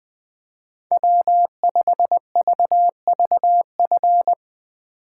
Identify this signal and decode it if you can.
Morse code, heard as a keyed tone, about 20 wpm, 700 Hz. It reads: W5VVF